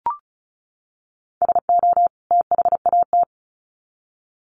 Morse code, heard as a keyed tone, about 34 wpm, 700 Hz. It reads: SO THAT
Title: Sets of 2 Words 35wpm